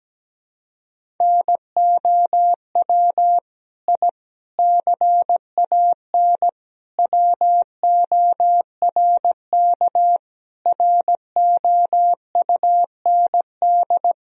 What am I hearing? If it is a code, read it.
Morse code, heard as a keyed tone, about 17 wpm, 700 Hz. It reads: NOW I CAN WORK ROUND
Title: Sentences from Top 1000 Words - Review Part 1 - 17wpm